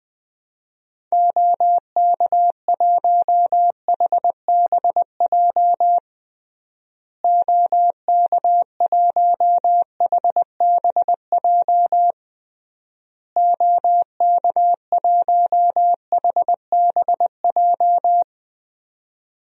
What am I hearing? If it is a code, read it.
Morse code, heard as a keyed tone, about 19 wpm, 700 Hz. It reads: OK1HBJ OK1HBJ OK1HBJ
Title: DXCC-100 Call Signs - 6 Characters - Repeated 3x - 20wpm